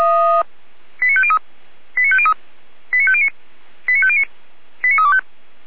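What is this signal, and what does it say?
Five-tone selective call (ZVEI), heard as quick runs of five tones, tones of about 70 ms. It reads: 89503 89503 89509 89509 89325
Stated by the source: ZVEI five-tone call received by SDR.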